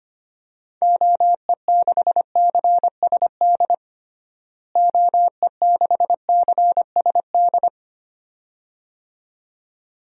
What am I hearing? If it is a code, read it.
Morse code, heard as a keyed tone, about 25 wpm, 700 Hz. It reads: OE6CSD OE6CSD
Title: DXCC-100 Call Signs - 6 Characters - Repeated 2x - 25wpm